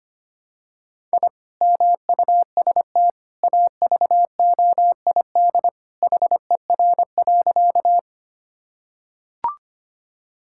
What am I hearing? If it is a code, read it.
Morse code, heard as a keyed tone, about 25 wpm, 700 Hz. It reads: I MUST AVOID HER.
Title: Sentences from Top 1000 Words - Review Part 6 - 25wpm